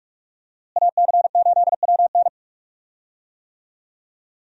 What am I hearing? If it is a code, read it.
Morse code, heard as a keyed tone, about 44 wpm, 700 Hz. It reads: AC8WN
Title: USA Call Signs - 5 Characters - Repeated 2x - 45wpm